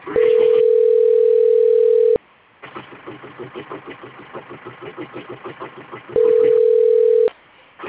Ringback tone (North America), 440 + 480 Hz.